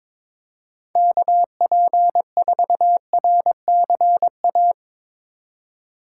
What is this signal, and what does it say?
Morse code, heard as a keyed tone, about 25 wpm, 700 Hz. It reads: KP4RCA
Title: Speed Racing - USA 2x3 Call Signs 22wpm